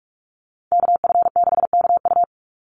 Morse code, heard as a keyed tone, about 44 wpm, 700 Hz. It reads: KF6KU
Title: USA Call Signs 45wpm